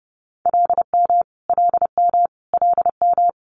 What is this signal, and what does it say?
Morse code, heard as a keyed tone, about 30 wpm, 700 Hz. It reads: LM LM LM
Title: Multi-letter Phonemes Repeated 3x 30wpm